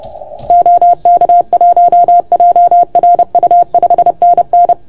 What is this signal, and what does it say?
Morse code, heard as a keyed tone, about 30 wpm, 690 Hz. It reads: OK1JRU 5NN
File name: Archiv / CUC381 / OK1FGD CUC 381